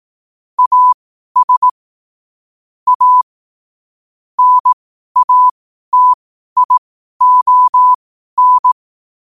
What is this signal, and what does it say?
Morse code, heard as a keyed tone, about 18 wpm, spaced as for 12 wpm, 1000 Hz. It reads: AS A NATION
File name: qotdfiles / 12wpm